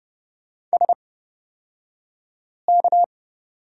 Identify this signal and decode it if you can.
Morse code, heard as a keyed tone, about 30 wpm, spaced as for 5 wpm, 700 Hz. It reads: SK
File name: course / 30:10wpm / 235 Lesson 38d - Letter q - Two Characters 30wpm